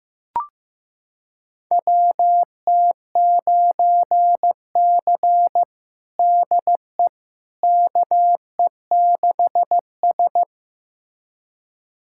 Morse code, heard as a keyed tone, about 15 wpm, 700 Hz. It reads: WT9C DE KE6S
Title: USA Callsign DE Callsign - 4 Characters - 15wpm